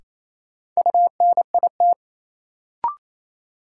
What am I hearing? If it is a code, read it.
Morse code, heard as a keyed tone, about 28 wpm, 700 Hz. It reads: UNIT
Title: Sentences from Top 1000 Words - Review Part 2 - New Word Drill - 28wpm